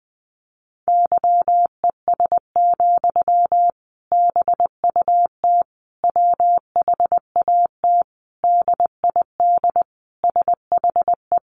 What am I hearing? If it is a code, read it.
Morse code, heard as a keyed tone, about 20 wpm, 700 Hz. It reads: YES, BUT WHAT DID SHE
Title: Sentences from Top 500 Words 20wpm